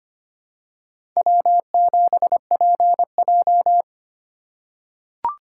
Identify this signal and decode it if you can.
Morse code, heard as a keyed tone, about 25 wpm, 700 Hz. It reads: W7PJ
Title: USA Call Signs - 4 Characters - 4x Character Spacing - 25wpm